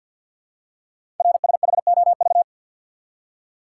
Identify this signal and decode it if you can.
Morse code, heard as a keyed tone, about 49 wpm, 700 Hz. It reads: RSHOV
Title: Code Groups - 5 Letter:Numbers - 50wpm